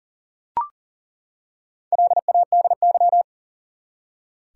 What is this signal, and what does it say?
Morse code, heard as a keyed tone, about 40 wpm, 700 Hz. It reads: LADY
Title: Four Letter Words - 40wpm